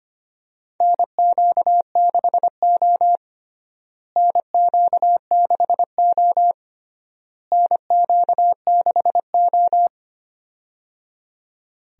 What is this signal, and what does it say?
Morse code, heard as a keyed tone, about 25 wpm, 700 Hz. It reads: NQ6O NQ6O NQ6O
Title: USA Call Signs - 4 Characters - Repeated 3x - 25wpm